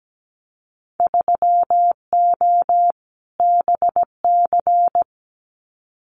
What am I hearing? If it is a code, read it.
Morse code, heard as a keyed tone, about 17 wpm, 700 Hz. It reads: 3O BC
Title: Winter Field Day Exchanges Encore 17wpm